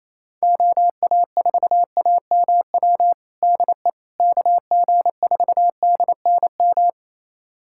Morse code, heard as a keyed tone, about 28 wpm, 700 Hz. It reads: OA4AMW DE KG4DNM
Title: USA Callsign DE Callsign 28wpm